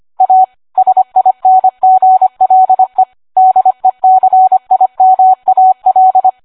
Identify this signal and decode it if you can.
Morse code, heard as a keyed tone, about 25 wpm, 750 Hz. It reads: A SINGLE DECIMAL